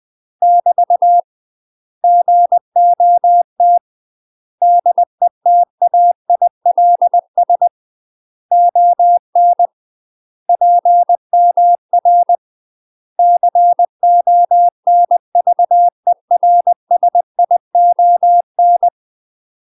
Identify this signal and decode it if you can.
Morse code, heard as a keyed tone, about 20 wpm, 700 Hz. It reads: <BT> GOT DETAILS ON PMR CONVERSION
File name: UK MASTER QSOs TRAINING / 20WPM / UK MASTER QSOs 0047